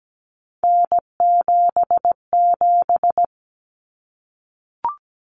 Morse code, heard as a keyed tone, about 17 wpm, 700 Hz. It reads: N77
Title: Common Cut Numbers - 3-Digit 17wpm